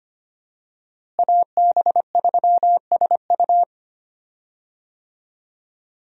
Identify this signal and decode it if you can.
Morse code, heard as a keyed tone, about 25 wpm, 700 Hz. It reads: AB3SU
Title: USA Call Signs - 5 Characters - Repeated 3x - 25wpm